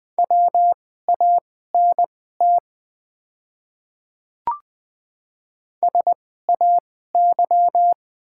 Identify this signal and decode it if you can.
Morse code, heard as a keyed tone, about 21 wpm, spaced as for 15 wpm, 700 Hz. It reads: WANT SAY
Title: 2x Character Spacing - Top 500 Words - 20wpm